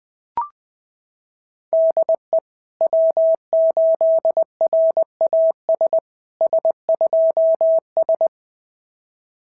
Morse code, heard as a keyed tone, about 20 wpm, 690 Hz. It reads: DE W8RAS S2S
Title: SOTA QSO Practice 20wpm